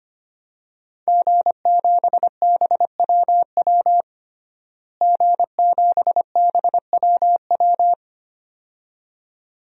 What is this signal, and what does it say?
Morse code, heard as a keyed tone, about 25 wpm, 700 Hz. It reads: G7BWW G7BWW
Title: DXCC-100 Call Signs - 5 Characters - Repeated 2x - 25wpm